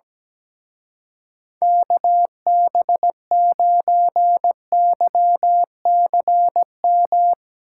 Morse code, heard as a keyed tone, about 17 wpm, 700 Hz. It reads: KB9YCM
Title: Speed Racing - USA Call Signs 17wpm